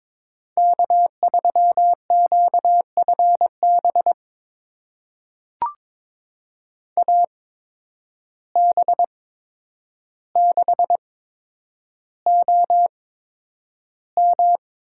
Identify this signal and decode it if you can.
Morse code, heard as a keyed tone, about 22 wpm, spaced as for 9 wpm, 700 Hz. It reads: K3QFB AB6OM
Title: USA Call Signs - 5 Characters - 8x Character Spacing - 22wpm